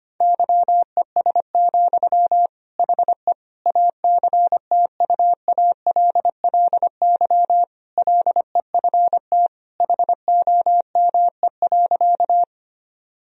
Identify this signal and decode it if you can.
Morse code, heard as a keyed tone, about 25 wpm, 700 Hz. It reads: YES, HE ACTUALLY LEFT HOME.
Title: Sentences from Top 1000 Words - Review Part 4 - 25wpm